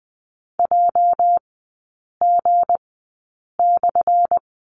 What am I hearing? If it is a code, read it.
Morse code, heard as a keyed tone, about 20 wpm, spaced as for 8 wpm, 700 Hz. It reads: JG/
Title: 218 Lesson 35e - Number 0 - Three Characters 20wpm